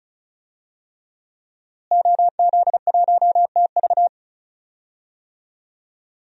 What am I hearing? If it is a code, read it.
Morse code, heard as a keyed tone, about 34 wpm, 700 Hz. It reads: OZ1TV
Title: DXCC-100 Call Signs - 5 Characters - Repeated 3x - 35wpm